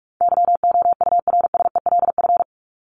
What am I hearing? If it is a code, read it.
Morse code, heard as a keyed tone, about 44 wpm, 700 Hz. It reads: YOURSELF
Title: Top 501-750 Words 45wpm